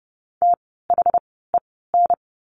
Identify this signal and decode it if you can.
Morse code, heard as a keyed tone, about 30 wpm, spaced as for 17 wpm, 700 Hz. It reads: THEN
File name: course / 30:10wpm / 230 Lesson 37f - Letter x - Words 30wpm